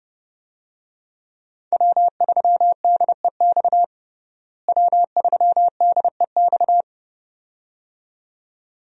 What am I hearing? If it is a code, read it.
Morse code, heard as a keyed tone, about 30 wpm, 700 Hz. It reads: W3DEX W3DEX
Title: USA Call Signs - 5 Characters - Repeated 2x - 30wpm